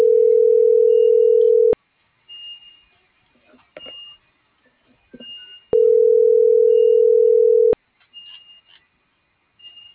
Ringback tone (North America), 440 + 480 Hz.